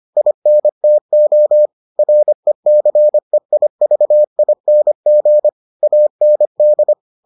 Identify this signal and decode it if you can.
Morse code, heard as a keyed tone, about 25 wpm, 600 Hz. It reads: INTO RECEIVING AND